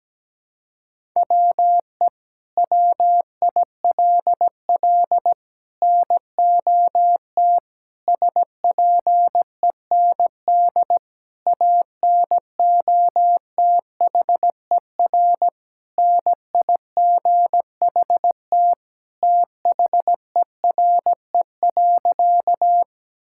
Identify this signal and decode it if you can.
Morse code, heard as a keyed tone, about 17 wpm, 700 Hz. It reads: WE WILL NOT SPEND ANOTHER NIGHT THERE.
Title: Sentences from Top 1000 Words - Review Part 5 - 17wpm